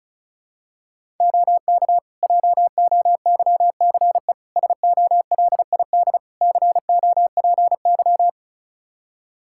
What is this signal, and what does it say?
Morse code, heard as a keyed tone, about 36 wpm, 700 Hz. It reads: OK JOYCE SOLID COPY
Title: Practice QSO - Exchange 3 - More Info - 35wpm